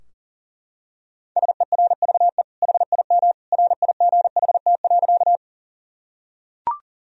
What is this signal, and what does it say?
Morse code, heard as a keyed tone, about 40 wpm, 700 Hz. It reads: SERVE HIM RIGHT.
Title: Sentences from Top 1000 Words - Review Part 2 - 40wpm